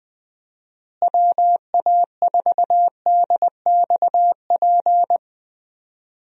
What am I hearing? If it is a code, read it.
Morse code, heard as a keyed tone, about 20 wpm, 700 Hz. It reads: WA4DXP
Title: USA Call Signs - 6 Characters - Repeated 3x - 20wpm